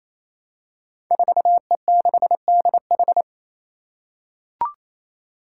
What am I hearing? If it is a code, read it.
Morse code, heard as a keyed tone, about 28 wpm, 700 Hz. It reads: 4E6DH
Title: Five Challenging Letter-Number Combinations 28wpm